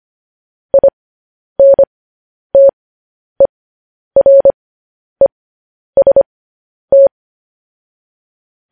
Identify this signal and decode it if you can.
Morse code, heard as a keyed tone, about 25 wpm, spaced as for 10 wpm, 550 Hz. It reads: INTEREST
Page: news-headlines-in-morse-code-at-10-wpm